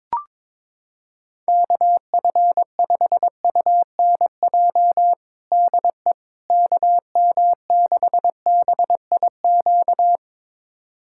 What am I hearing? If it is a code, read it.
Morse code, heard as a keyed tone, about 22 wpm, 700 Hz. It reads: KF5UNJ DE KM6BIQ
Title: USA Callsign DE Callsign 22wpm